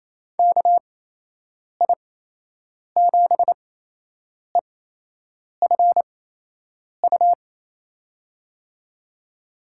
Morse code, heard as a keyed tone, about 28 wpm, spaced as for 8 wpm, 700 Hz. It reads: KI7EFU
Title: USA Call Signs - 6 Characters - 8x Character Spacing - 28wpm